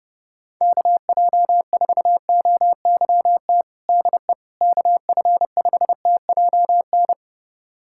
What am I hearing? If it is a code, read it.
Morse code, heard as a keyed tone, about 32 wpm, 700 Hz. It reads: KJ4OYT DE KF5TJN
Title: USA Callsign DE Callsign - 6 Characters - 30wpm